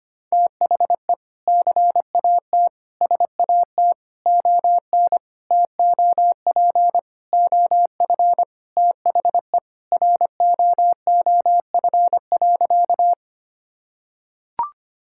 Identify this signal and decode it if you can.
Morse code, heard as a keyed tone, about 25 wpm, 700 Hz. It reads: THE CAT SAT ON TOP OF THE ROOF.
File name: practice / 25wpm / Sentences with 3-Word Phrases - 25wpm